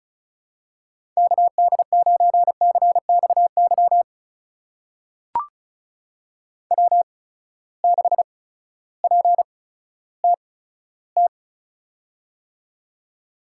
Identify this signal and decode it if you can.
Morse code, heard as a keyed tone, about 36 wpm, spaced as for 16 wpm, 700 Hz. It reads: KD9CXY W6PTT
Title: USA Call Signs - 8x Character Spacing - 35wpm